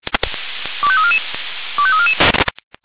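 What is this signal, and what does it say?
Five-tone selective call (ZVEI), heard as quick runs of five tones, tones of about 70 ms. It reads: 25344 25344
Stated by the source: ZVEI five-tone selective call received on an SDR.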